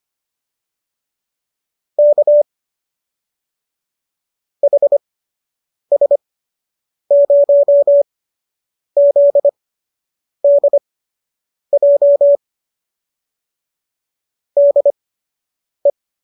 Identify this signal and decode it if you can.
Morse code, heard as a keyed tone, about 25 wpm, spaced as for 8 wpm, 570 Hz. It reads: K HS0ZDJ DE